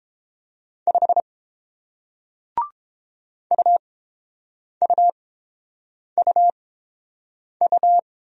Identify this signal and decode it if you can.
Morse code, heard as a keyed tone, about 30 wpm, spaced as for 7 wpm, 700 Hz. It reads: 5 UUUU